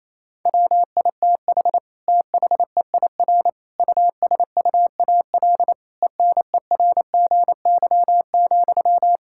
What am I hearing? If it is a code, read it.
Morse code, heard as a keyed tone, about 27 wpm, 700 Hz. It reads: WITH THEIR USUAL ENERGY,
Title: Sentences from Top 1000 Words - Review Part 6 - 28wpm